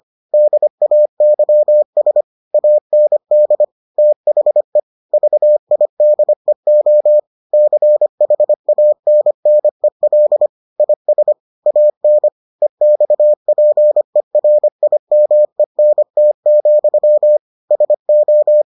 Morse code, heard as a keyed tone, about 25 wpm, 600 Hz. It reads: DAYS AND THE VIDEO CHANNEL IS AN EXPERIMENT, SO